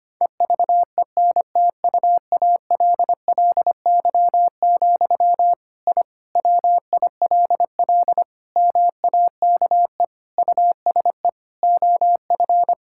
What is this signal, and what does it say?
Morse code, heard as a keyed tone, about 25 wpm, 700 Hz. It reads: EVENTUALLY, I WILL MAKE USE OF
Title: Sentences from Top 1000 Words - Review Part 6 - 25wpm